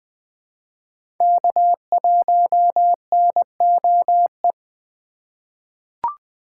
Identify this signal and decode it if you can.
Morse code, heard as a keyed tone, about 20 wpm, 700 Hz. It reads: K1NOE
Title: USA Call Signs - 5 Characters - 2x Character Spacing - 20wpm